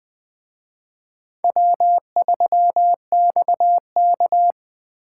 Morse code, heard as a keyed tone, about 20 wpm, 700 Hz. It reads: W3XK
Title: Speed Racing - USA 1x2 Call Signs 17wpm